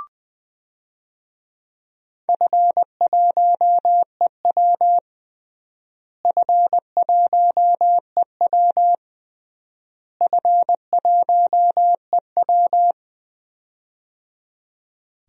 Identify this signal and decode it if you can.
Morse code, heard as a keyed tone, about 21 wpm, 700 Hz. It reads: F1EW F1EW F1EW
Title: DXCC-100 Call Signs - 4 Characters - Repeated 3x - 20wpm